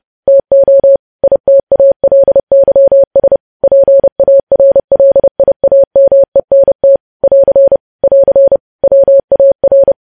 Morse code, heard as a keyed tone, about 30 wpm, 550 Hz. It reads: TO ITALYS PARLIAMENT <AR> <AR> WAR
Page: news headlines in morse code at 30 wpm